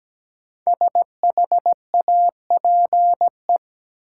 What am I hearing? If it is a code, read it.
Morse code, heard as a keyed tone, about 17 wpm, 700 Hz. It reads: SHAPE
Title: Sentences from Top 500 Words - New Word Drill - 17wpm